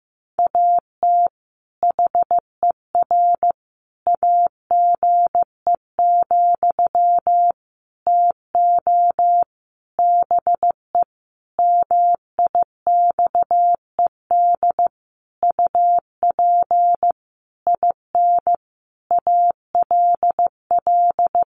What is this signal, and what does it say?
Morse code, heard as a keyed tone, about 15 wpm, 700 Hz. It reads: AT HER AGE, TO BE MIXED UP IN ALL
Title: Sentences from Top 1500 Words 15wpm